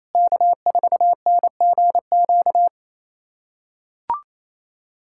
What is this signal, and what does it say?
Morse code, heard as a keyed tone, about 28 wpm, 700 Hz. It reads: K4NGQ
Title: USA Callsign DE Callsign - 5 Characters - 28wpm